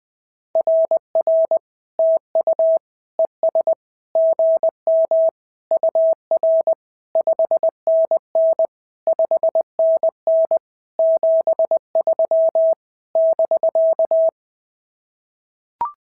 Morse code, heard as a keyed tone, about 19 wpm, 650 Hz. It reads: RR TU ES GM UR 5NN 5NN 73 <BK>
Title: SOTA QSO Practice 20wpm